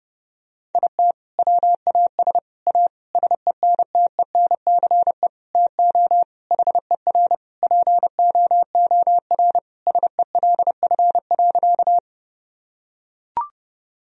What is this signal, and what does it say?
Morse code, heard as a keyed tone, about 32 wpm, 700 Hz. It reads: IT WAS A SENTENCE TO HER POOR SELF.
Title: Sentences from Top 1000 Words - Review Part 1 - 30wpm